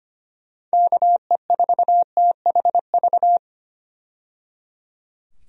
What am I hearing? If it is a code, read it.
Morse code, heard as a keyed tone, about 25 wpm, 700 Hz. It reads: KE4THV
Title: USA Call Signs - 6 Characters - 25wpm